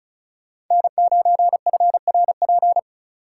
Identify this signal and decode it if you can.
Morse code, heard as a keyed tone, about 37 wpm, 700 Hz. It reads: N9FRP
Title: Speed Racing - USA 1x3 Call Signs 30wpm